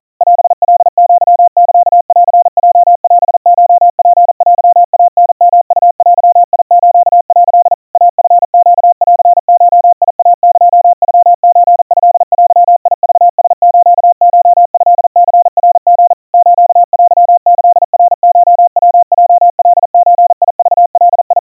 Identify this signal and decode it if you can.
Morse code (Japanese Wabun code), heard as a keyed tone, about 40 wpm, 700 Hz. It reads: オカアサンヲオコツテイタヨウデス」 イチネンゴイエノソトデクラスコトニナリ ステキナコヤヲカゾクガ
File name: kawa / Morse sample / kamyu 150 cpm 700 Hz Crispy